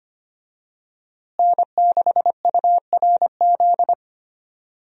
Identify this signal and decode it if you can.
Morse code, heard as a keyed tone, about 25 wpm, 700 Hz. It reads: N6URZ
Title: USA Call Signs - 5 Characters - Repeated 2x - 25wpm